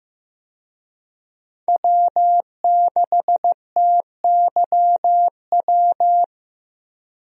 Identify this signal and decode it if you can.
Morse code, heard as a keyed tone, about 15 wpm, 700 Hz. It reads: W6TYW
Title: USA Call Signs - 5 Characters - Repeated 2x - 15wpm